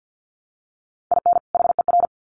Morse code, heard as a keyed tone, about 50 wpm, 700 Hz. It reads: IN HER
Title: Sets of 2 Words 50wpm